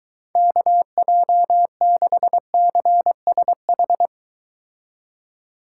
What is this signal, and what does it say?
Morse code, heard as a keyed tone, about 23 wpm, 700 Hz. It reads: KJ6CSH